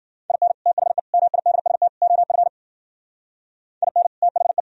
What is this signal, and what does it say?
Morse code, heard as a keyed tone, about 51 wpm, 700 Hz. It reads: IN THE MIDST OF IN THE